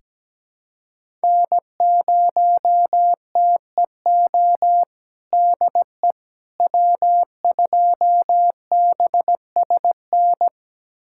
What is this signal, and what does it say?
Morse code, heard as a keyed tone, about 17 wpm, 700 Hz. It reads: N0TEO DE W2BSN
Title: USA Callsign DE Callsign - 5 Characters - 17wpm